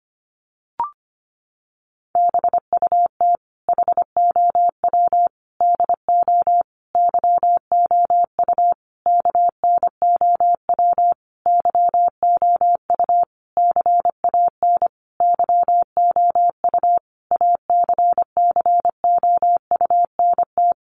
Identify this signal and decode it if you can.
Morse code, heard as a keyed tone, about 25 wpm, 700 Hz. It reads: BUT HOW DO YOU KNOW YOU CAN YOU ACCOUNT
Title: Sentences from Top 900 Words 25wpm